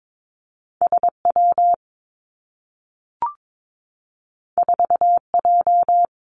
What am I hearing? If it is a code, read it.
Morse code, heard as a keyed tone, about 22 wpm, 700 Hz. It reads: S W 4 J
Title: Two Challenging Letter-Number Combinations 22wpm